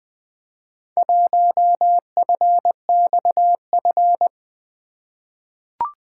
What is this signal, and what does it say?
Morse code, heard as a keyed tone, about 20 wpm, 700 Hz. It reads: 1FXF
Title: Four Letter-Number Combinations - 20wpm